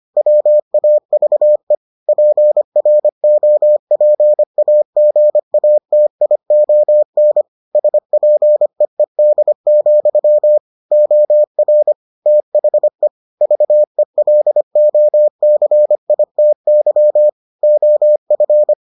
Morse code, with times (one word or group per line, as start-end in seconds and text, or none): 0.2-1.7 WAVE
2.1-7.4 PROPAGATION
7.7-10.6 SPEED,
10.9-11.9 OR
12.3-13.1 THE
13.4-17.3 VELOCITY
17.6-18.7 OF